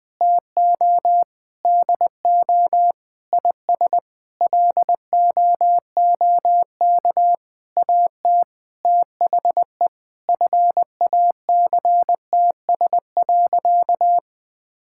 Morse code, heard as a keyed tone, about 20 wpm, 700 Hz. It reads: TO DO IS LOOK AT THE FACTS.
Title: Sentences with 8-Word Phrases - Spoken Beginning - 20wpm